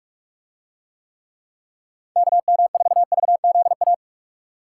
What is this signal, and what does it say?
Morse code, heard as a keyed tone, about 44 wpm, 700 Hz. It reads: KM4VZA